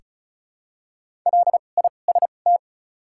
Morse code, heard as a keyed tone, about 36 wpm, spaced as for 25 wpm, 700 Hz. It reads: LIST
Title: 2x Character Spacing - Top 500 Words - 35wpm